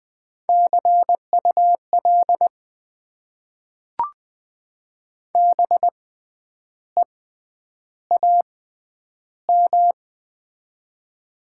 Morse code, heard as a keyed tone, about 20 wpm, spaced as for 9 wpm, 700 Hz. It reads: CUL BEAM
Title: Top 90 QSO Elements - 6x Character Spacing - 20wpm